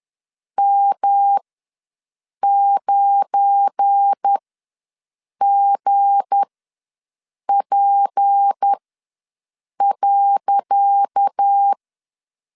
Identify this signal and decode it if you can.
Morse code, heard as a keyed tone, about 10 wpm, spaced as for 6 wpm, 800 Hz. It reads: M9GP.
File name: LEZIONI / 10 SIGNS / 10C DOT ws 10wpm 4213